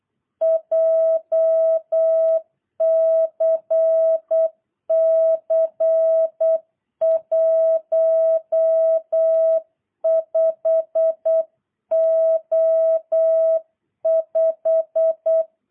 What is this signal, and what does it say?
Morse code, heard as a keyed tone, about 8 wpm, 650 Hz. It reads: JCC15O5
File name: jh8cbh / 2022cw